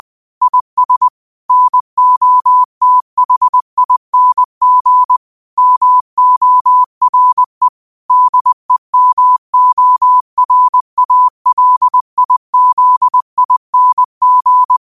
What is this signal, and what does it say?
Morse code, heard as a keyed tone, about 20 wpm, 1000 Hz. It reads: IS NOTHING MORE DEMORALIZING